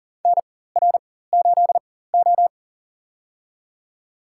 Morse code, heard as a keyed tone, about 40 wpm, spaced as for 19 wpm, 700 Hz. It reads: NR8O
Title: USA Call Signs - 4 Characters - 4x Character Spacing - 40wpm